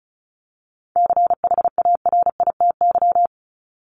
Morse code, heard as a keyed tone, about 34 wpm, 700 Hz. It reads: CHARITY
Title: Top 1501-1750 Words 35wpm